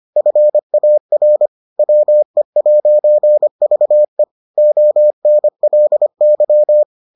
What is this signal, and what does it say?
Morse code, heard as a keyed tone, about 25 wpm, 600 Hz. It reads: FAR WE'VE ONLY